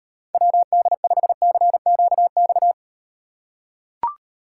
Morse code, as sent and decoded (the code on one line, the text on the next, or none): .-- -.. ..... -.-. --.- -..-
WD5CQX